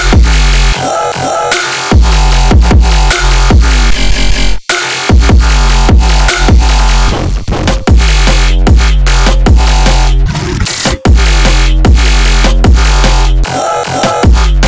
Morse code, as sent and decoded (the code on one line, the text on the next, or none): none